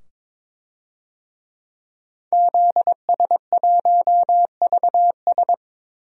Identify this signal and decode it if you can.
Morse code, heard as a keyed tone, about 22 wpm, 700 Hz. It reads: ZS1VS
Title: DXCC-100 Call Signs - 5 Characters - 22wpm